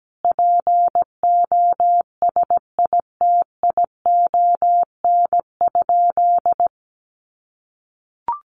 Morse code, heard as a keyed tone, about 17 wpm, 700 Hz. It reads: POSITION?
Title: Sentences from Top 1400 Words 17wpm